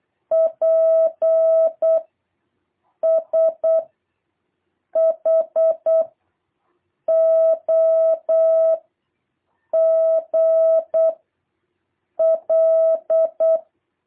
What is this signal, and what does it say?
Morse code, heard as a keyed tone, about 8 wpm, 660 Hz. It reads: P S H O G L
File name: jh8cbh / 2023cw